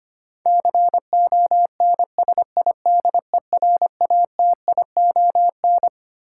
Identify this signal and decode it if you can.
Morse code, heard as a keyed tone, about 25 wpm, 700 Hz. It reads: CONSIDERATION
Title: Sentences from Top 1500 Words - New Word Drill - 25wpm